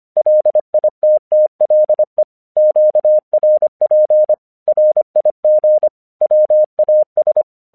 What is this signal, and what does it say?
Morse code, heard as a keyed tone, about 25 wpm, 610 Hz. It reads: LITTLE QRP RIG WAS